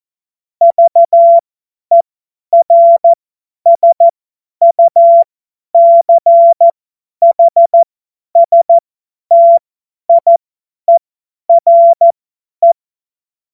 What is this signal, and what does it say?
Morse code, heard as a keyed tone, about 14 wpm, spaced as for 10 wpm, 680 Hz. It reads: VERSUCHSTIERE
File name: Morsekurs / Witze / Witz 02. August 2022 50 F70